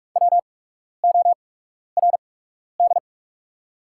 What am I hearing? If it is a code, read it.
Morse code, heard as a keyed tone, about 44 wpm, spaced as for 12 wpm, 700 Hz. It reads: WORD
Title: 8x Character Spacing - Top 200 Words - 45wpm